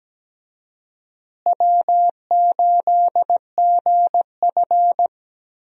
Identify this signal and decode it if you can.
Morse code, heard as a keyed tone, about 17 wpm, 700 Hz. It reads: W8GF
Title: USA Call Signs - 4 Characters - Repeated 2x - 17wpm